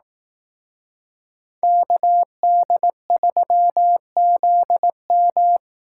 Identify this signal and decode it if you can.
Morse code, heard as a keyed tone, about 18 wpm, 700 Hz. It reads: KD3ZM
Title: Speed Racing - USA 2x2 Call Signs 15wpm